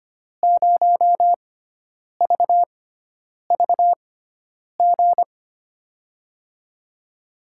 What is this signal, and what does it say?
Morse code, heard as a keyed tone, about 25 wpm, spaced as for 9 wpm, 700 Hz. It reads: 0VVG
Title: USA Call Signs - 6 Characters - 6x Character Spacing - 25wpm